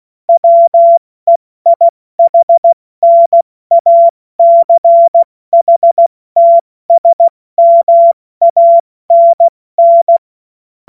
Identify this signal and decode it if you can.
Morse code, heard as a keyed tone, about 16 wpm, 680 Hz. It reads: WEIHNACHTSMANN
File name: Morsekurs / Witze / Witz 09. November 2021, 70 F80